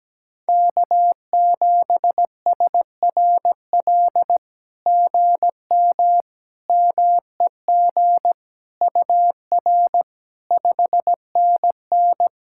Morse code, heard as a keyed tone, about 17 wpm, 700 Hz. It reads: K7SRL GM MEG UR 5NN
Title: SOTA QSO Practice 17wpm